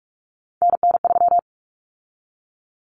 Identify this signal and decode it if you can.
Morse code, heard as a keyed tone, about 44 wpm, 700 Hz. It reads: NN3
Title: Common Cut Numbers - 3-Digit 45wpm